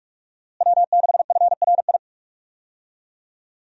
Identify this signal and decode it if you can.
Morse code, heard as a keyed tone, about 43 wpm, 700 Hz. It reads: W6FRI
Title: USA Call Signs - 5 Characters - Repeated 3x - 45wpm